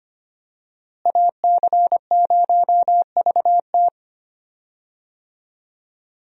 Morse code, heard as a keyed tone, about 25 wpm, 700 Hz. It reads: AC0VT
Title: USA Call Signs - 5 Characters - Repeated 2x - 25wpm